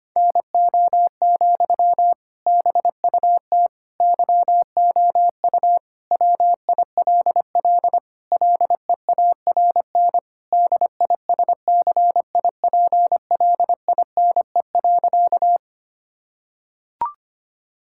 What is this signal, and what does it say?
Morse code, heard as a keyed tone, about 25 wpm, 700 Hz. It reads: NO, BUT YOU WILL LEARN DISCIPLINE.
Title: Sentences from Top 1000 Words - Review Part 6 - 25wpm